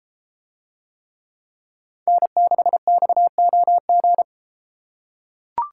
Morse code, heard as a keyed tone, about 32 wpm, 700 Hz. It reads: N6XOG